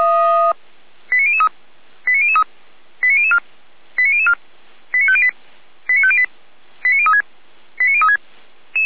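Five-tone selective call (ZVEI), heard as quick runs of five tones, tones of about 70 ms. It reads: 89003 89003 89004 89004 89508 89508 89036 89036
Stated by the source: ZVEI five-tone selective call received by SDR, feuerwehr-orlamuende.org fire brigade Orlamünde.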